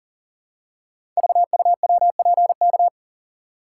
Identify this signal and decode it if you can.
Morse code, heard as a keyed tone, about 40 wpm, 700 Hz. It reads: VUWPK
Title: Code Groups - 5 Letters - 40wpm